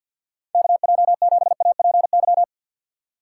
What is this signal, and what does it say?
Morse code, heard as a keyed tone, about 50 wpm, 700 Hz. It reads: KJ7APY
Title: USA Call Signs - 6 Characters - 3x Character Spacing - 50wpm